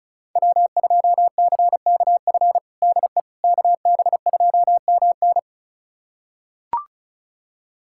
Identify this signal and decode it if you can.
Morse code, heard as a keyed tone, about 36 wpm, 700 Hz. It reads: W2CKF DE KB2MN
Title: USA Callsign DE Callsign - 5 Characters - 35wpm